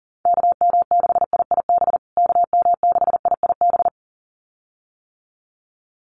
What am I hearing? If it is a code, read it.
Morse code, heard as a keyed tone, about 40 wpm, 700 Hz. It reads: KM6IIB KM6IIB
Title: USA Call Signs Repeated 2x 40wpm